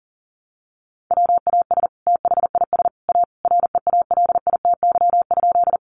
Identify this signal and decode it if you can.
Morse code, heard as a keyed tone, about 40 wpm, 700 Hz. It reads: WAS THIS A REALITY?
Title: Sentences from Top 1200 Words 40wpm